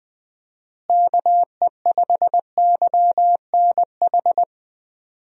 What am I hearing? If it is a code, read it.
Morse code, heard as a keyed tone, about 20 wpm, 700 Hz. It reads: KE5YNH
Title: USA Call Signs - 6 Characters - Repeated 3x - 20wpm